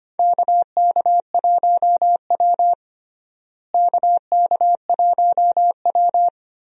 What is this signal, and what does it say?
Morse code, heard as a keyed tone, about 25 wpm, 700 Hz. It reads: KK1W KK1W
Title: USA Call Signs - 4 Characters - Repeated 3x - 25wpm